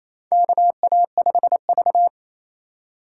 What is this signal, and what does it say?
Morse code, heard as a keyed tone, about 28 wpm, 700 Hz. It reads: KA5V
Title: USA Call Signs - 4 Characters - 8x Character Spacing - 28wpm